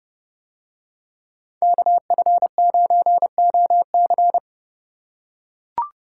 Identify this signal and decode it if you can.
Morse code, heard as a keyed tone, about 27 wpm, 700 Hz. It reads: KF9OC